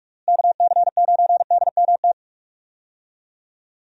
Morse code, heard as a keyed tone, about 47 wpm, 700 Hz. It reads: KC9DMT
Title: USA Call Signs - 6 Characters - Repeated 2x - 45wpm